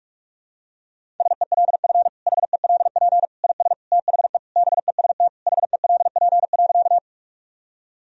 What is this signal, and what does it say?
Morse code, heard as a keyed tone, about 47 wpm, 700 Hz. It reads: SELF HELP IS THE BEST HELP.
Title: Sentences from Top 1000 Words - Review Part 1 - 45wpm